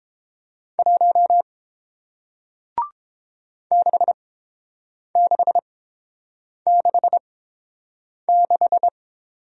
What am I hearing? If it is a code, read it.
Morse code, heard as a keyed tone, about 30 wpm, spaced as for 7 wpm, 700 Hz. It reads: 1 6666